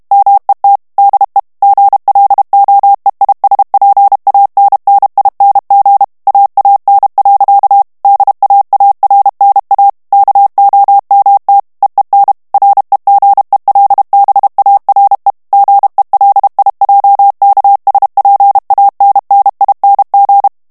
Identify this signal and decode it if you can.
Morse code, heard as a keyed tone, about 32 wpm, 800 Hz. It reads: MET DE GLOEISPANNING AAN. DAARNA KOMT EEN REGELBARE GELIJKSPANNING
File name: HOLLANDAIS texte clair 2 vitesse 30 mots minute